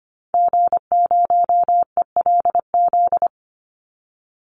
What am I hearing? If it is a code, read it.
Morse code, heard as a keyed tone, about 25 wpm, 700 Hz. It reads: G0ELZ
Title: Call Signs - 25wpm